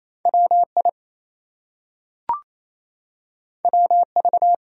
Morse code, heard as a keyed tone, about 28 wpm, 700 Hz. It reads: WI WV
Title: US States and Canadian Provinces 28wpm